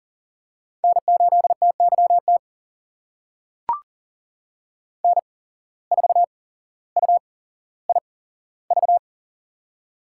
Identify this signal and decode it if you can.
Morse code, heard as a keyed tone, about 40 wpm, spaced as for 17 wpm, 700 Hz. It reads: N8TYT N4UIV
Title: USA Call Signs - 5 Characters - 8x Character Spacing - 40wpm